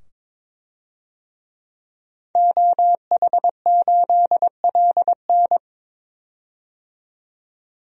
Morse code, heard as a keyed tone, about 22 wpm, 700 Hz. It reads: OH8LN
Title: DXCC-100 Call Signs - 5 Characters - 22wpm